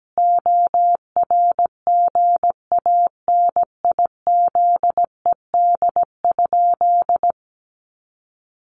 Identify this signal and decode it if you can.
Morse code, heard as a keyed tone, about 17 wpm, 700 Hz. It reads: ORGANIZED?
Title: Sentences from Top 1300 Words 17wpm